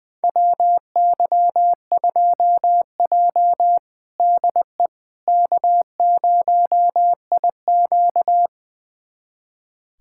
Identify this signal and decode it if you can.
Morse code, heard as a keyed tone, about 20 wpm, 700 Hz. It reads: WY2J DE K0IQ
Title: USA Callsign DE Callsign - 4 Characters - 20wpm